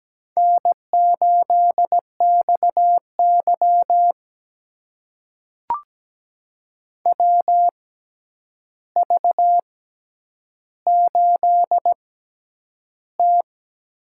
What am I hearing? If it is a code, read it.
Morse code, heard as a keyed tone, about 17 wpm, spaced as for 9 wpm, 700 Hz. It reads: N8XY WV8T
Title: USA Call Signs - 4 Characters - 6x Character Spacing - 17wpm